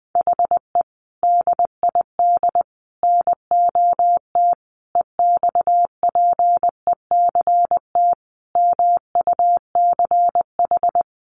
Morse code, heard as a keyed tone, about 20 wpm, 700 Hz. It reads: HE DID NOT EXPECT MUCH
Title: Sentences from Top 1100 Words 20wpm